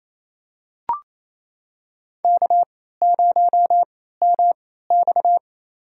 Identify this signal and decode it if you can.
Morse code, heard as a keyed tone, about 28 wpm, spaced as for 16 wpm, 700 Hz. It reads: K0MX
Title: USA Call Signs - 4 Characters - 3x Character Spacing - 28wpm